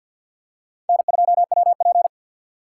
Morse code, heard as a keyed tone, about 53 wpm, 700 Hz. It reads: N2WP
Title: USA Call Signs - 4x Character Spacing - 50wpm